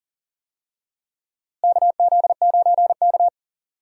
Morse code, heard as a keyed tone, about 40 wpm, 700 Hz. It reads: KZ9K
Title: USA Call Signs - 4 Characters - Repeated 3x - 40wpm